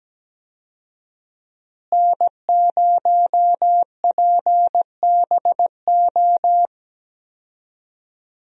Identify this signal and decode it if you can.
Morse code, heard as a keyed tone, about 17 wpm, 700 Hz. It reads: N0PBO
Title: USA Call Signs - 5 Characters - Repeated 2x - 17wpm